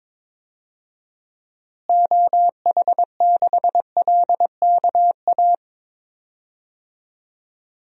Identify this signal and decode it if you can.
Morse code, heard as a keyed tone, about 22 wpm, 700 Hz. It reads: OH6LKA
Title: DXCC-100 Call Signs - 6 Characters - Repeated 3x - 22wpm